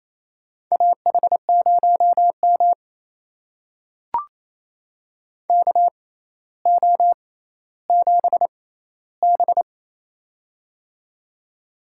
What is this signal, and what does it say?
Morse code, heard as a keyed tone, about 28 wpm, spaced as for 14 wpm, 700 Hz. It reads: AH0M KO7B
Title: USA Call Signs - 4 Characters - 6x Character Spacing - 28wpm